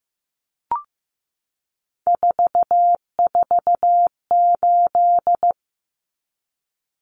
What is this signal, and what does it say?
Morse code, heard as a keyed tone, about 15 wpm, 700 Hz. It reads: 448
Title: Numbers - 3-Digit 15wpm